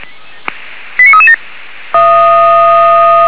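Five-tone selective call (ZVEI), heard as quick runs of five tones, tones of about 70 ms. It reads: 89337
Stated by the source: Five-tone ZVEI selective call 89335, received on SDR.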